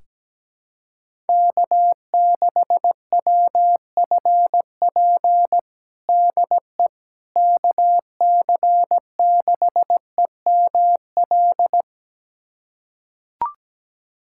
Morse code, heard as a keyed tone, about 17 wpm, 700 Hz. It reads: K6WFP DE KC6EML
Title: USA Callsign DE Callsign 17wpm